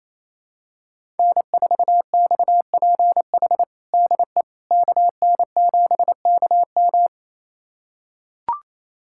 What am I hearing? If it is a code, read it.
Morse code, heard as a keyed tone, about 28 wpm, 700 Hz. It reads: N4XPH DE KN7KM
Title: USA Callsign DE Callsign - 5 Characters - 28wpm